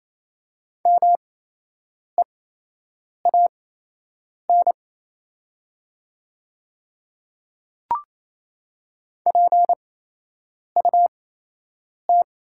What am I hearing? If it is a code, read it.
Morse code, heard as a keyed tone, about 28 wpm, spaced as for 8 wpm, 700 Hz. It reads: MEAN PUT
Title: 8x Character Spacing - Top 200 Words - 28wpm